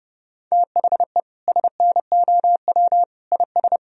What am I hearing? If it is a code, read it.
Morse code, heard as a keyed tone, about 30 wpm, 700 Hz. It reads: THE SNOW IS
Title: Sentences from Top 1000 Words - Review Part 3 - 30wpm